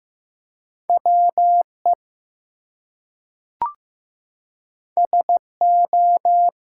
Morse code, heard as a keyed tone, about 15 wpm, 700 Hz. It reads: WE SO
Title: Two Letter Words - 15wpm